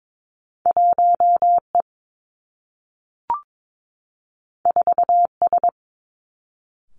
Morse code, heard as a keyed tone, about 22 wpm, 700 Hz. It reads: 1 E 4 S
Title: Two Challenging Letter-Number Combinations 22wpm